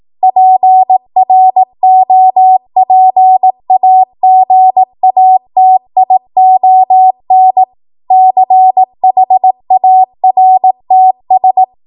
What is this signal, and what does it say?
Morse code, heard as a keyed tone, about 18 wpm, 750 Hz. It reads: PROPAGATION CHARTS
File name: Archive / 18 WPM / 18WPM